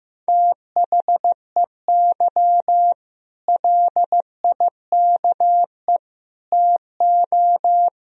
Morse code, heard as a keyed tone, about 15 wpm, 700 Hz. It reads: THEY LIKE TO
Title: Sentences from Top 1000 Words - Review Part 3 - 15wpm